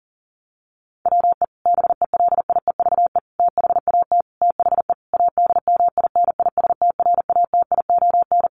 Morse code, heard as a keyed tone, about 40 wpm, 700 Hz. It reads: WE BELIEVE THAT THE ADMINISTRATION
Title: Sentences from Top 1300 Words 40wpm